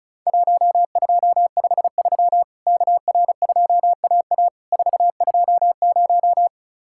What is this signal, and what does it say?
Morse code, heard as a keyed tone, about 34 wpm, 700 Hz. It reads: 1253 KR2AA 420
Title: Worked All Europe DX Contest - QTC Exchange - 35wpm